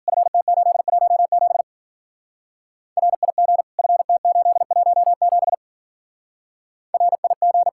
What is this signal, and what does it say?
Morse code, heard as a keyed tone, about 47 wpm, 700 Hz. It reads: FT817 RIG FT817 RIG
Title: Speed-Racing Common QSO Phrases Encore - 40wpm